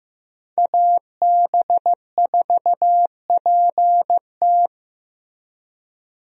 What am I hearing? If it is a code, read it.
Morse code, heard as a keyed tone, about 15 wpm, 700 Hz. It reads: AB4PT
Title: USA Call Signs - 5 Characters - Repeated 3x - 15wpm